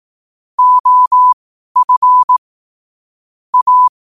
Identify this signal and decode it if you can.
Morse code, heard as a keyed tone, about 18 wpm, spaced as for 12 wpm, 1000 Hz. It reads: OF A
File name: qotdfiles / 12wpm